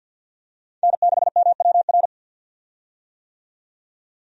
Morse code, heard as a keyed tone, about 53 wpm, 700 Hz. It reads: N6MWR
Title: USA Call Signs - 5 Characters - 50wpm